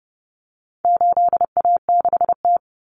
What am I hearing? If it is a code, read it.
Morse code, heard as a keyed tone, about 30 wpm, 700 Hz. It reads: 8A6T
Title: Common Cut Numbers - 4-Digit 30wpm